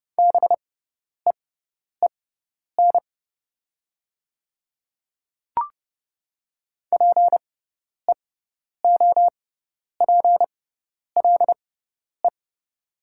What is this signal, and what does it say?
Morse code, heard as a keyed tone, about 30 wpm, spaced as for 10 wpm, 700 Hz. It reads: BEEN PEOPLE
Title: 6x Character Spacing - Top 100 Words - 30wpm